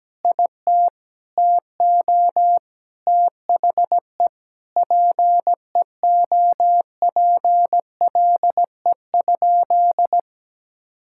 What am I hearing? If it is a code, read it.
Morse code, heard as a keyed tone, about 17 wpm, 700 Hz. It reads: IT TO THE PEOPLE?
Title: Sentences from Top 1000 Words - Review Part 5 - 17wpm